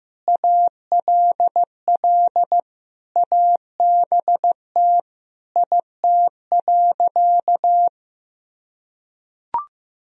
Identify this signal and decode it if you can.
Morse code, heard as a keyed tone, about 15 wpm, 700 Hz. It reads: ALL ABT IT.
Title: Sentences from Top 200 Words and Abbreviations 15wpm